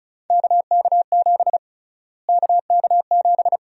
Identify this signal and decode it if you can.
Morse code, heard as a keyed tone, about 34 wpm, 700 Hz. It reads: KK7 KK7
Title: USA Call Sign Prefixes - Repeated 2x - 35wpm